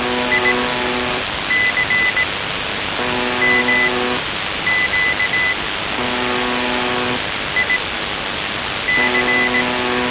Morse code, heard as a keyed tone, about 18 wpm, spaced as for 7 wpm, 2000 Hz, with noise. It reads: ICMQ IC
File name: s28-4625usb- - -cw-bykompot